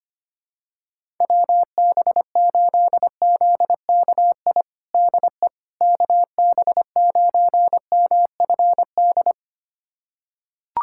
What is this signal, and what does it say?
Morse code, heard as a keyed tone, about 25 wpm, 700 Hz. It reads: WB8ZKI DE KB9MFD
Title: USA Callsign DE Callsign - 6 Characters - 25wpm